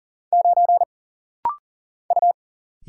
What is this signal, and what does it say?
Morse code, heard as a keyed tone, about 40 wpm, 700 Hz. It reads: 9 U
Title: Single Letter-Number Alt - Rapid-Fire 40wpm